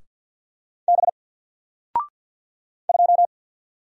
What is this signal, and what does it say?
Morse code, heard as a keyed tone, about 48 wpm, 700 Hz. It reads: B 2
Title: Single Letter-Number Alt 50wpm